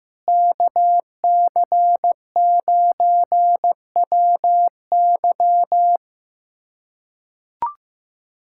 Morse code, heard as a keyed tone, about 15 wpm, 700 Hz. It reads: KC9WY
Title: USA Callsign DE Callsign 15wpm